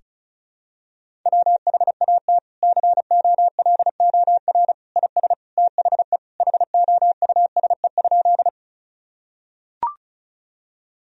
Morse code, heard as a keyed tone, about 36 wpm, 700 Hz. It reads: WHAT COLOR IS THE HOUSE?
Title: Sentences from Top 1000 Words - Review Part 2 - 35wpm